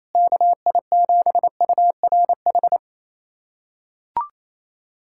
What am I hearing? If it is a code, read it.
Morse code, heard as a keyed tone, about 28 wpm, 700 Hz. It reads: KI7URH
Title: USA Callsign DE Callsign - 6 Characters - 28wpm